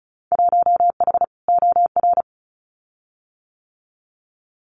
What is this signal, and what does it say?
Morse code, heard as a keyed tone, about 34 wpm, 700 Hz. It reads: 1H OR
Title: Winter Field Day Exchanges 35wpm